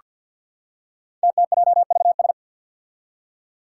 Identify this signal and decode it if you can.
Morse code, heard as a keyed tone, about 47 wpm, 700 Hz. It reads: TTJVS
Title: Five Challenging Letter-Number Combinations 50wpm